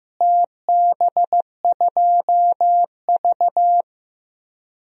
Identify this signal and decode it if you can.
Morse code, heard as a keyed tone, about 15 wpm, 700 Hz. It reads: TB2V
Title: DXCC-100 Call Signs - 4 Characters - Repeated 3x - 15wpm